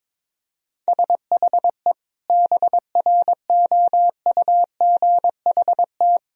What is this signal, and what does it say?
Morse code, heard as a keyed tone, about 22 wpm, 700 Hz. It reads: SHE BROUGHT
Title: Sentences from Top 1000 Words - Review Part 4 - 22wpm